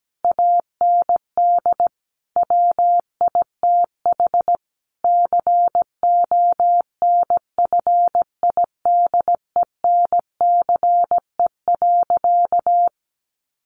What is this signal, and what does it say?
Morse code, heard as a keyed tone, about 17 wpm, 700 Hz. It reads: AND WITH CONFIDENCE.
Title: Sentences from Top 1500 Words 17wpm